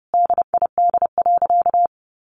Morse code, heard as a keyed tone, about 30 wpm, 700 Hz. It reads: DID.
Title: Sentences from Top 100 Words 30wpm